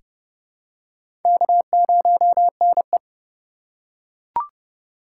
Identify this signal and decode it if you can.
Morse code, heard as a keyed tone, about 29 wpm, 700 Hz. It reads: K0NE
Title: USA Call Signs - 4 Characters - 4x Character Spacing - 30wpm